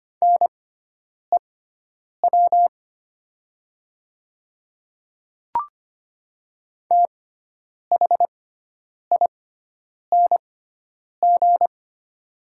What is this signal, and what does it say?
Morse code, heard as a keyed tone, about 25 wpm, spaced as for 9 wpm, 690 Hz. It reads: NEW THING
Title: 6x Character Spacing - Top 300 Words - 25wpm